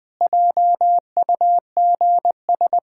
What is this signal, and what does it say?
Morse code, heard as a keyed tone, about 19 wpm, 700 Hz. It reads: JUGS
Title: Pangram Challenge v2 - 20wpm